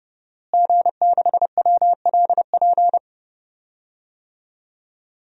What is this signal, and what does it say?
Morse code, heard as a keyed tone, about 30 wpm, 700 Hz. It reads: G6WLP
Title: DXCC-100 Call Signs - 5 Characters - Repeated 2x - 30wpm